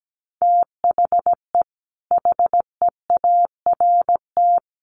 Morse code, heard as a keyed tone, about 17 wpm, 700 Hz. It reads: THE HEART
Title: Sets of 3 Words 17wpm